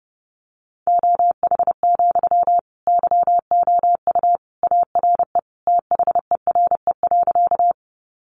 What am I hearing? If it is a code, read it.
Morse code, heard as a keyed tone, about 30 wpm, 700 Hz. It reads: OH, YOU ARE THERE.
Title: Sentences from Top 500 Words 30wpm